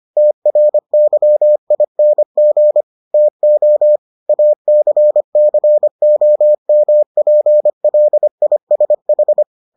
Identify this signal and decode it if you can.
Morse code, heard as a keyed tone, about 25 wpm, 600 Hz. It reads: TRYING TO ACCOMPLISH